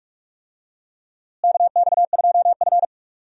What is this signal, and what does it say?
Morse code, heard as a keyed tone, about 42 wpm, 700 Hz. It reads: KX2F
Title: USA Call Signs - 4 Characters - Repeated 2x - 45wpm